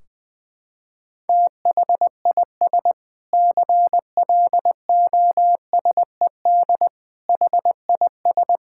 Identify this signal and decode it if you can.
Morse code, heard as a keyed tone, about 21 wpm, 700 Hz. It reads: THIS CLOSED HIS
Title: Sentences from Top 1600 Words 20wpm